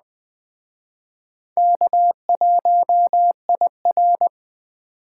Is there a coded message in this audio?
Yes: Morse code from 1.6 to 4.3 s, heard as a keyed tone, about 21 wpm, 700 Hz. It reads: K1IR